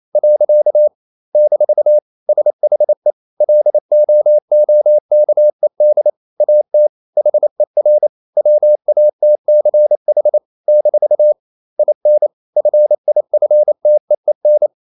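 Morse code, heard as a keyed tone, about 28 wpm, 600 Hz. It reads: . - SHE LOOKED AT HER WATCH - IN FIFTEEN
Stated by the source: Morse code practice at 28 wpm, text of Ken Follett's novel Never in English, from CW med Gnister.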